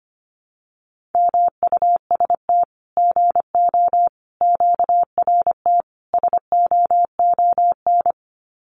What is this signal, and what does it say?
Morse code, heard as a keyed tone, about 25 wpm, 700 Hz. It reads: MUST GO QRT SOON
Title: Common QSO Phrases 25wpm